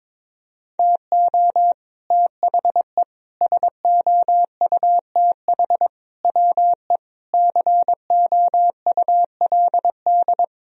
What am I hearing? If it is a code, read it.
Morse code, heard as a keyed tone, about 22 wpm, 700 Hz. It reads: TO THE SOUTH WE COULD
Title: Sentences from Top 1000 Words - Review Part 2 - 22wpm